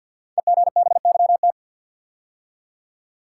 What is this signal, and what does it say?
Morse code, heard as a keyed tone, about 52 wpm, 700 Hz. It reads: EGBYT
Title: Code Groups - 5 Letter:Numbers - 50wpm